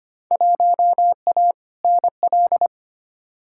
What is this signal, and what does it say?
Morse code, heard as a keyed tone, about 25 wpm, 700 Hz. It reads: 1A NL
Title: Field Day Exchanges Encore - 25wpm